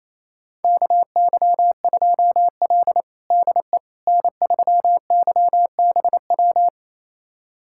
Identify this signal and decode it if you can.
Morse code, heard as a keyed tone, about 28 wpm, 700 Hz. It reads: KY2L DE N3YBW
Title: USA Callsign DE Callsign 28wpm